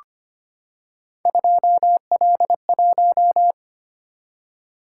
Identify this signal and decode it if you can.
Morse code, heard as a keyed tone, about 25 wpm, 700 Hz. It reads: 2L1
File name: Three Letter-Number Combinations - 25wpm